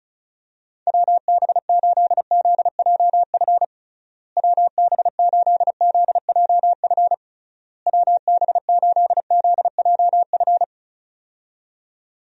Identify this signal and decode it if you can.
Morse code, heard as a keyed tone, about 34 wpm, 700 Hz. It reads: WB8ZJF WB8ZJF WB8ZJF
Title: USA Call Signs - 6 Characters - Repeated 3x - 35wpm